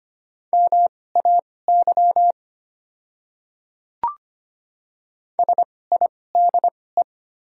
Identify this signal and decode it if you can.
Morse code, heard as a keyed tone, about 25 wpm, spaced as for 18 wpm, 690 Hz. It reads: MAY SIDE
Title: 2x Character Spacing - Top 300 Words - 25wpm